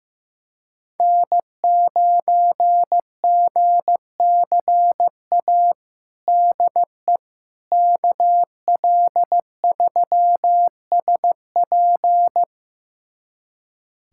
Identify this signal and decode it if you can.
Morse code, heard as a keyed tone, about 15 wpm, 700 Hz. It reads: N9GCA DE KL3SP
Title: USA Callsign DE Callsign - 5 Characters - 15wpm